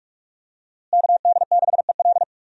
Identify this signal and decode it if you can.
Morse code, heard as a keyed tone, about 42 wpm, 700 Hz. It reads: KD6EL
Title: USA Call Signs - 5 Characters - Repeated 2x - 45wpm